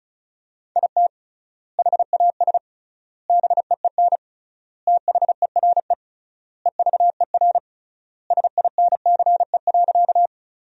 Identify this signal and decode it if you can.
Morse code, heard as a keyed tone, about 34 wpm, 700 Hz. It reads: IT HAS BEEN THERE EVER SINCE.
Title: 3x Word-Spacing - Sentences from Top 400 Words - 35wpm